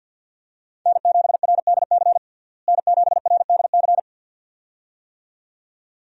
Morse code, heard as a keyed tone, about 50 wpm, 700 Hz. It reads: N7RDC N7RDC
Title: USA Call Signs - 5 Characters - Repeated 2x - 50wpm